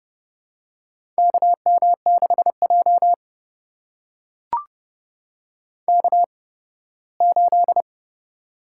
Morse code, heard as a keyed tone, about 30 wpm, spaced as for 18 wpm, 700 Hz. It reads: KM6J K8
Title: USA Call Signs - 4 Characters - 8x Character Spacing - 30wpm